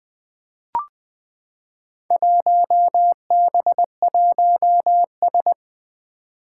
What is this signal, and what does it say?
Morse code, heard as a keyed tone, about 20 wpm, 700 Hz. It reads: 1B1S
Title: Four Challenging Letter-Number Combinations 20wpm